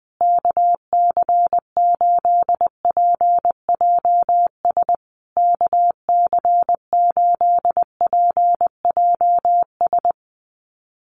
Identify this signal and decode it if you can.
Morse code, heard as a keyed tone, about 20 wpm, 700 Hz. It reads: KC8PJS KC8PJS
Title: USA Call Signs Repeated 3x 20wpm